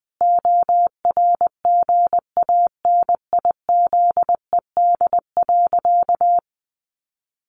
Morse code, heard as a keyed tone, about 20 wpm, 700 Hz. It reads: ORGANIZED.
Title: Sentences from Top 1300 Words 20wpm